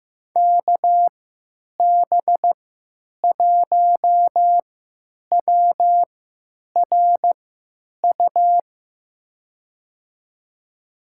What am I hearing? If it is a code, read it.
Morse code, heard as a keyed tone, about 16 wpm, spaced as for 9 wpm, 700 Hz. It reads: KB1WRU
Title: USA Call Signs - 6 Characters - 3x Character Spacing - 15wpm